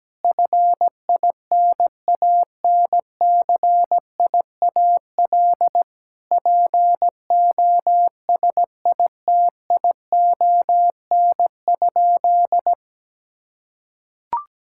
Morse code, heard as a keyed tone, about 17 wpm, 700 Hz. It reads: FINANCIAL POSITION?
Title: Sentences from Top 1000 Words - Review Part 5 - 17wpm